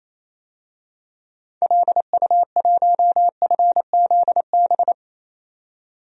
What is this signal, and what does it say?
Morse code, heard as a keyed tone, about 28 wpm, 700 Hz. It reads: LU1FZB
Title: DXCC-100 Call Signs - 6 Characters - 28wpm